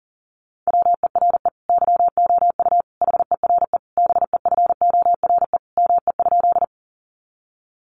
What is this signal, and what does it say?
Morse code, heard as a keyed tone, about 40 wpm, 700 Hz. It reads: WERE YOU HERE BEFORE ME?
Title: Sentences from Top 200 Words 40wpm